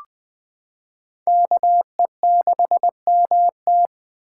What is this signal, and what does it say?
Morse code, heard as a keyed tone, about 20 wpm, 700 Hz. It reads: KE6MT
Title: 2023 SOTA Chaser Call Signs - 20wpm